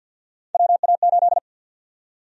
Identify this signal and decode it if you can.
Morse code, heard as a keyed tone, about 52 wpm, 700 Hz. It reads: WA8
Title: USA Call Sign Prefixes - Repeated 2x - 50wpm